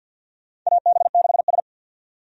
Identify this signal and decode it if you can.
Morse code, heard as a keyed tone, about 52 wpm, 700 Hz. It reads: AB6S
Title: USA Call Signs - 4 Characters - 50wpm